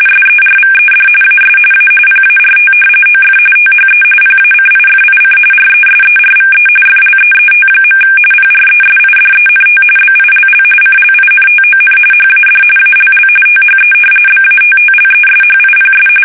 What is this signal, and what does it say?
RTTY, about 45.45 baud. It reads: SERVED AS A PART OF THE PACIFIC RESERVE FLEET. PRESIDENT BILL CLINTON HAD ALL THE BATTLESHIPS
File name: fsk850 / battleshipnj